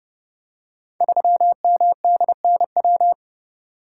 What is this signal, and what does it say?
Morse code, heard as a keyed tone, about 30 wpm, 700 Hz. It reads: 3MDNW
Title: Code Groups - 5 Letter:Numbers - 30wpm